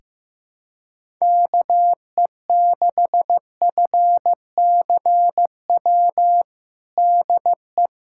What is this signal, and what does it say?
Morse code, heard as a keyed tone, about 15 wpm, 700 Hz. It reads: KE6FCW DE
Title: USA Callsign DE Callsign 15wpm